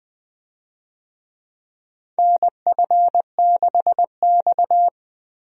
Morse code, heard as a keyed tone, about 20 wpm, 700 Hz. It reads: NF6X
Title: USA Call Signs - 4 Characters - 20wpm